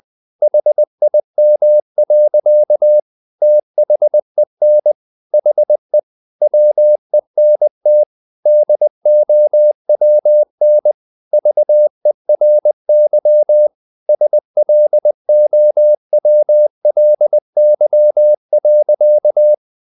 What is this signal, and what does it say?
Morse code, heard as a keyed tone, about 20 wpm, 600 Hz. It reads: HIM. THEN HE WENT DOWN VERY SLOWLY.
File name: the old man and the sea part 2 20wpm